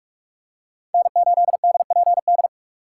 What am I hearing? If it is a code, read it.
Morse code, heard as a keyed tone, about 44 wpm, 700 Hz. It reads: N8DPD
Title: K1USN SST QSO Practice - 45wpm